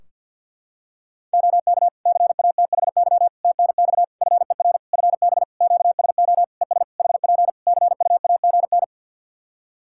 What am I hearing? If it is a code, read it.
Morse code, heard as a keyed tone, about 50 wpm, 700 Hz. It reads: OK CATHY TNX FER FB QSO ES HP CUAGN
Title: K1USN SST QSO Practice 50wpm